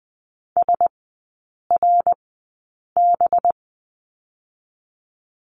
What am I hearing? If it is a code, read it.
Morse code, heard as a keyed tone, about 20 wpm, spaced as for 8 wpm, 700 Hz. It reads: SRB
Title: 184 Lesson 30e - Letter b - Three Characters 20wpm